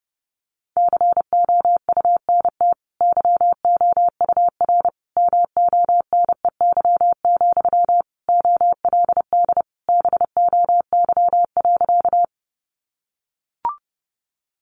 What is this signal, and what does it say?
Morse code, heard as a keyed tone, about 30 wpm, 700 Hz. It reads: COUNT YOUR MONEY, OLD BOY.
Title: Sentences from Top 1500 Words 30wpm